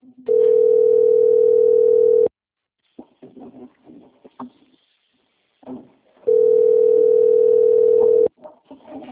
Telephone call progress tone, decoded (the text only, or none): ringback tone